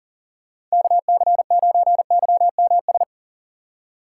Morse code, heard as a keyed tone, about 40 wpm, 700 Hz. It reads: KC9YMS
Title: USA Call Signs - 6x Character Spacing - 40wpm